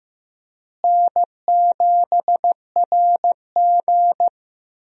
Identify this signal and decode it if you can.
Morse code, heard as a keyed tone, about 15 wpm, 700 Hz. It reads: N7RG
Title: USA Call Signs - 4 Characters - Repeated 2x - 15wpm